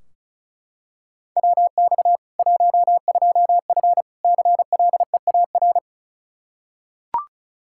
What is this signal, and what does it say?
Morse code, heard as a keyed tone, about 36 wpm, 700 Hz. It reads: WX 12F CLEAR
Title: Practice QSO - Exchange 2 - Info - 35wpm